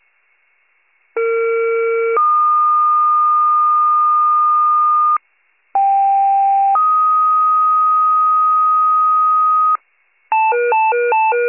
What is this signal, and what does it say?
Two-tone page, heard as a one-second tone, then a three-second tone, about 460 then 1220 Hz, about 790 then 1250 Hz. It is heard from 1.2 to 9.8 s.